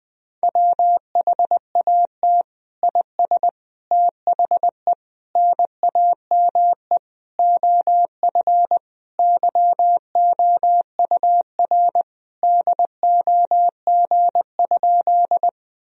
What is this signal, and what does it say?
Morse code, heard as a keyed tone, about 20 wpm, 700 Hz. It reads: WHAT IS THE NAME OF YOUR DOG?
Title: Sentences from Top 1000 Words - Review Part 2 - 20wpm